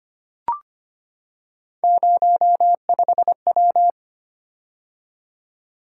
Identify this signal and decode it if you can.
Morse code, heard as a keyed tone, about 25 wpm, 700 Hz. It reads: 05W
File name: Three Letter-Number Combinations - 25wpm